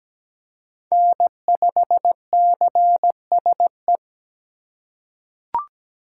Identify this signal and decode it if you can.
Morse code, heard as a keyed tone, about 17 wpm, 700 Hz. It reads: N5CSE
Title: USA Call Signs - 5 Characters - 4x Character Spacing - 17wpm